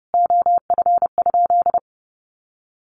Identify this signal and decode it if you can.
Morse code, heard as a keyed tone, about 30 wpm, 700 Hz. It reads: OF?
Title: Sentences from Top 1300 Words 30wpm